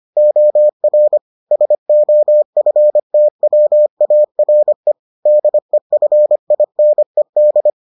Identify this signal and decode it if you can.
Morse code, heard as a keyed tone, about 25 wpm, 600 Hz. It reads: OR SOFTWARE DEFINED